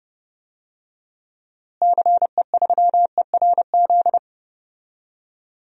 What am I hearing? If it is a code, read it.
Morse code, heard as a keyed tone, about 30 wpm, 700 Hz. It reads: CE3ERZ
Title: DXCC-100 Call Signs - 6 Characters - Repeated 3x - 30wpm